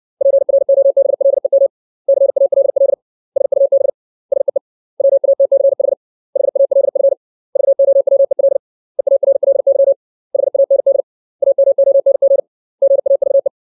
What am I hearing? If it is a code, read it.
Morse code, heard as a keyed tone, about 60 wpm, 550 Hz. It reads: PROBLEM CALL SUB SEE WATCH HALF VOWEL EARLY HAND AMONG GAVE
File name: 632 words / 632 Words 60WPM